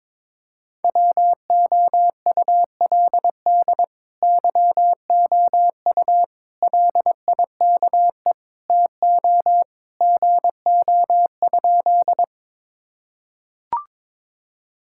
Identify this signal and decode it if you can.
Morse code, heard as a keyed tone, about 22 wpm, 700 Hz. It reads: WOULD YOU LIKE TO GO?
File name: practice / 22wpm / Sentences with 2-Word Phrases - 22wpm